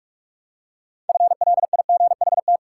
Morse code, heard as a keyed tone, about 41 wpm, 700 Hz. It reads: FLIGHT
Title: Six Letter Words - 45wpm